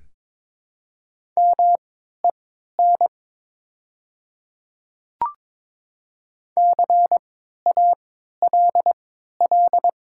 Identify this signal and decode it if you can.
Morse code, heard as a keyed tone, about 22 wpm, spaced as for 13 wpm, 700 Hz. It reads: MEN CALL
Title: 3x Character Spacing - Top 300 Words - 22wpm